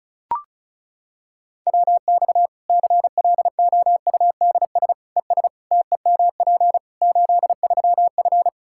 Morse code, heard as a keyed tone, about 36 wpm, 700 Hz. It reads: WX CLOUDS ES TEMP 83F
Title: K1USN SST QSO Practice 35wpm